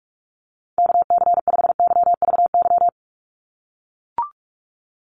Morse code, heard as a keyed tone, about 44 wpm, 700 Hz. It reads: KC5YVY
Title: USA Call Signs 45wpm